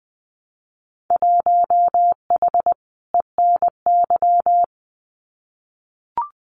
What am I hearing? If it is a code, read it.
Morse code, heard as a keyed tone, about 20 wpm, 700 Hz. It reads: 1H ENY
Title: Winter Field Day Exchanges 20wpm